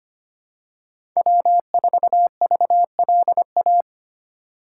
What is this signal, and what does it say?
Morse code, heard as a keyed tone, about 25 wpm, 700 Hz. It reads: W4VLA
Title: USA Call Signs - 5 Characters - 8x Character Spacing - 25wpm